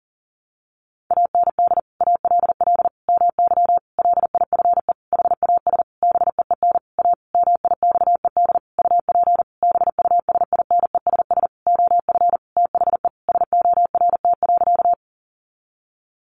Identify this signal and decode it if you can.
Morse code, heard as a keyed tone, about 40 wpm, 700 Hz. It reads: AND ALL MY LIFE HAS BEEN A MIXED UP BUSINESS OF THE SORT.
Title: Sentences from Top 1500 Words 40wpm